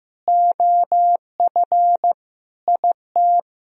Morse code, heard as a keyed tone, about 15 wpm, 700 Hz. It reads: OF IT
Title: Sentences from Top 1000 Words - Review Part 2 - 15wpm